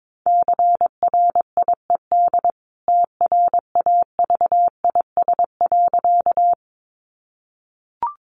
Morse code, heard as a keyed tone, about 22 wpm, 700 Hz. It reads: CRIED TRAVIS.
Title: A Sound Of Thunder 22wpm